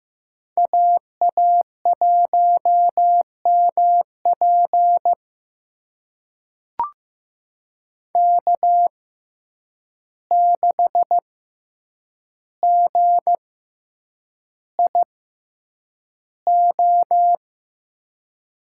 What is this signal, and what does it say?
Morse code, heard as a keyed tone, about 15 wpm, spaced as for 8 wpm, 700 Hz. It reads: AA1MP K6GIO
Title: USA Call Signs - 5 Characters - 6x Character Spacing - 15wpm